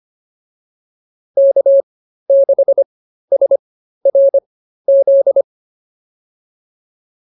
Morse code, heard as a keyed tone, about 25 wpm, spaced as for 13 wpm, 550 Hz. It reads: K6SRZ